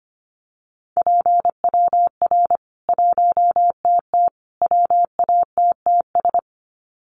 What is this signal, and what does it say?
Morse code, heard as a keyed tone, about 25 wpm, 700 Hz. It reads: PWR 1TT WATTS
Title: Common QSO Phrases 25wpm